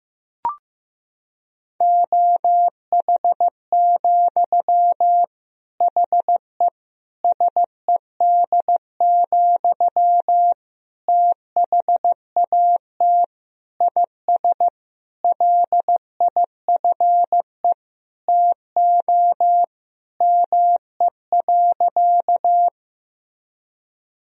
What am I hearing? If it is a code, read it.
Morse code, heard as a keyed tone, about 14 wpm, 700 Hz. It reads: OH, HE SED, THAT IS LIFE TO ME.
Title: Sentences from Top 500 Words and Abbreviations 15wpm